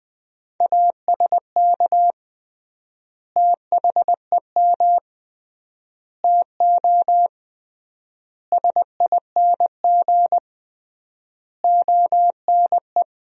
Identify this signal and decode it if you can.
Morse code, heard as a keyed tone, about 20 wpm, 700 Hz. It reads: ASK THEM TO SING ONE
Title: 3x Word-Spacing - Sentences from Top 400 Words - 20wpm